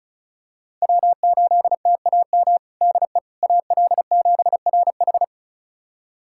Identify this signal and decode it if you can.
Morse code, heard as a keyed tone, about 36 wpm, 700 Hz. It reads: W8TAM DE AL7RH
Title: USA Callsign DE Callsign - 5 Characters - 35wpm